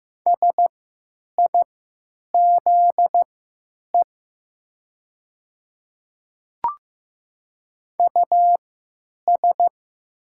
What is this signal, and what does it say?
Morse code, heard as a keyed tone, about 15 wpm, spaced as for 9 wpm, 700 Hz. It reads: SIZE US
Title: 3x Character Spacing - Top 500 Words - 15wpm